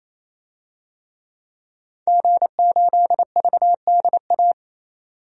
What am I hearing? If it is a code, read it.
Morse code, heard as a keyed tone, about 28 wpm, 700 Hz. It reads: G8VDA
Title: DXCC-100 Call Signs - 5 Characters - 28wpm